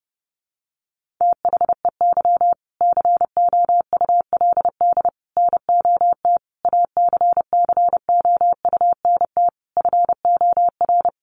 Morse code, heard as a keyed tone, about 30 wpm, 700 Hz. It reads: THEY COULD NOT ACCOUNT FOR
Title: Sentences from Top 900 Words 30wpm